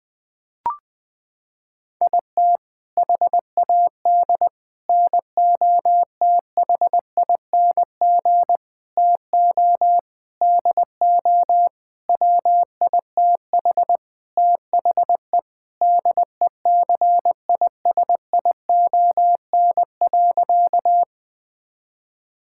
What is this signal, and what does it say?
Morse code, heard as a keyed tone, about 21 wpm, 700 Hz. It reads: IT HAD NOTHING TO DO WITH THE DECISION.
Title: Sentences with 6-Word Phrases - 20wpm